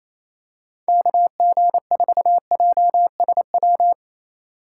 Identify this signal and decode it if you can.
Morse code, heard as a keyed tone, about 28 wpm, 700 Hz. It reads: KG4JSW
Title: USA Call Signs - 8x Character Spacing - 28wpm